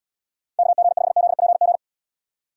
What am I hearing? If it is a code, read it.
Morse code, heard as a keyed tone, about 78 wpm, 700 Hz. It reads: KG5ZPG